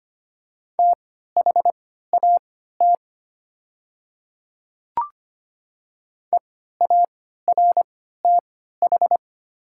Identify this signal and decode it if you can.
Morse code, heard as a keyed tone, about 25 wpm, spaced as for 14 wpm, 700 Hz. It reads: THAT EARTH
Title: 3x Character Spacing - Top 500 Words - 25wpm